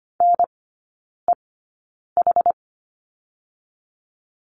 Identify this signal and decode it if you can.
Morse code, heard as a keyed tone, about 25 wpm, spaced as for 9 wpm, 700 Hz. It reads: NEH
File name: Lesson 14e - Number 2 - Three Characters 25wpm